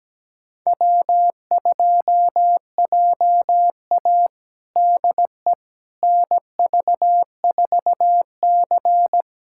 Morse code, heard as a keyed tone, about 17 wpm, 700 Hz. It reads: W2JA DE NV4C
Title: USA Callsign DE Callsign - 4 Characters - 17wpm